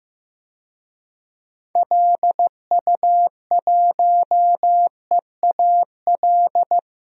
Morse code, heard as a keyed tone, about 15 wpm, 700 Hz. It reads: LU1EAL
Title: DXCC-100 Call Signs - 6 Characters - 15wpm